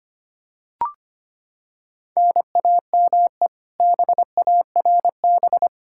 Morse code, heard as a keyed tone, about 25 wpm, 700 Hz. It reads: NAME BARB
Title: Practice QSO - Exchange 1 - Beginning - 25wpm